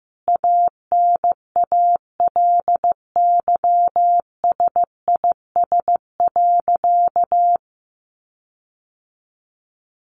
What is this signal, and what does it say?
Morse code, heard as a keyed tone, about 15 wpm, 700 Hz. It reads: ANALYSIS.
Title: Sentences from Top 1400 Words 15wpm